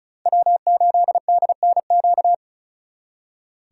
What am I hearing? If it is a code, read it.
Morse code, heard as a keyed tone, about 45 wpm, 700 Hz. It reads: W8DNQ